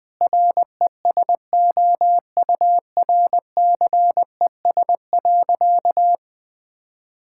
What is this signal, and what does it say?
Morse code, heard as a keyed tone, about 20 wpm, 700 Hz. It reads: RESOURCES.
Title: Sentences with 5-Word Phrases - 20wpm